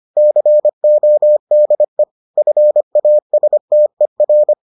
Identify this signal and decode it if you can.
Morse code, heard as a keyed tone, about 25 wpm, 600 Hz. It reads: CODE FASTER